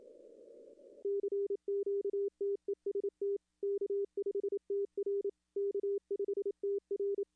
Morse code, heard as a keyed tone, about 28 wpm, 400 Hz. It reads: CQTEST K5TR K5TR
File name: contests / arrl-160-k5tr-op-w5tn -2300